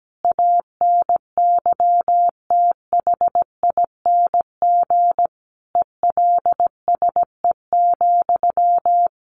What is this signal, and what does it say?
Morse code, heard as a keyed tone, about 17 wpm, 700 Hz. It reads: ANYTHING ELSE,
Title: Sentences from Top 1200 Words 17wpm